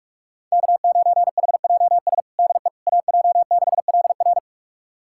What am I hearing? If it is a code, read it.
Morse code, heard as a keyed tone, about 42 wpm, 700 Hz. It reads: K9HJS DE AJ6LR
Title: USA Callsign DE Callsign - 5 Characters - 45wpm